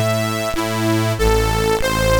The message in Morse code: none